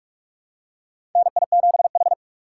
Morse code, heard as a keyed tone, about 41 wpm, 700 Hz. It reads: NI7H
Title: USA Call Signs - 4 Characters - 4x Character Spacing - 45wpm